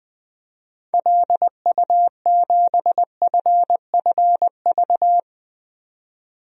Morse code, heard as a keyed tone, about 20 wpm, 700 Hz. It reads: LU7FFV
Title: DXCC-100 Call Signs - 6 Characters - 20wpm